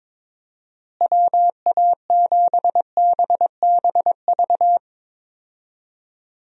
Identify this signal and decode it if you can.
Morse code, heard as a keyed tone, about 22 wpm, 700 Hz. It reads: WA7BBV
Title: USA Call Signs - 6 Characters - Repeated 3x - 22wpm